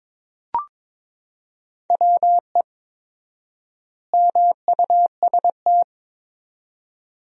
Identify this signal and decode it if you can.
Morse code, heard as a keyed tone, about 22 wpm, 700 Hz. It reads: WE MUST
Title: 4x Word-Spacing - Sentences from Top 400 Words - 22wpm